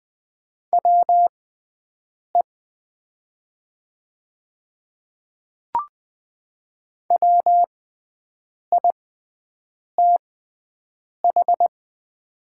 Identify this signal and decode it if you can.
Morse code, heard as a keyed tone, about 20 wpm, spaced as for 7 wpm, 700 Hz. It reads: WE WITH
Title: 6x Character Spacing - Top 100 Words - 20wpm